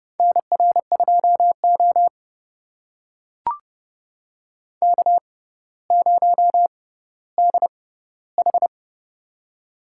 Morse code, heard as a keyed tone, about 30 wpm, spaced as for 15 wpm, 700 Hz. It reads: NR2O K0DH
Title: USA Call Signs - 4 Characters - 6x Character Spacing - 30wpm